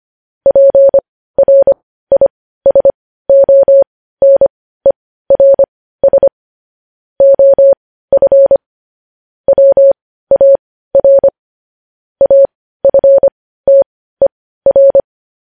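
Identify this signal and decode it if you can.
Morse code, heard as a keyed tone, about 25 wpm, spaced as for 15 wpm, 550 Hz. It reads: PRISONERS OF WAR AFTER
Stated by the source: Morse code news headlines sent at 15, opening with vvv.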